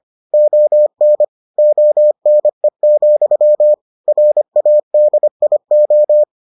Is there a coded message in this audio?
Yes: Morse code from 0.3 to 6.2 s, heard as a keyed tone, about 25 wpm, 600 Hz. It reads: ON ONE, RADIO